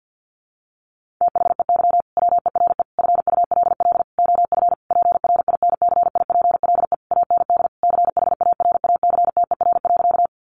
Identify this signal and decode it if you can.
Morse code, heard as a keyed tone, about 50 wpm, 700 Hz. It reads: THEY WERE FULL OF PRINCIPLE AND CHARACTER.
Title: Sentences from Top 1400 Words 50wpm